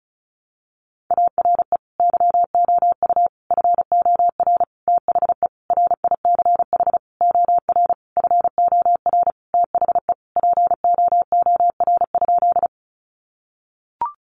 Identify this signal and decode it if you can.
Morse code, heard as a keyed tone, about 34 wpm, 700 Hz. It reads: ARE YOU FOR THE RICH OR FOR THE POOR?
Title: Sentences from Top 1100 Words 35wpm